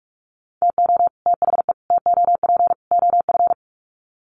Morse code, heard as a keyed tone, about 46 wpm, 700 Hz. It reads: TO THE TOP OF
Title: Sets of 4 Words 45wpm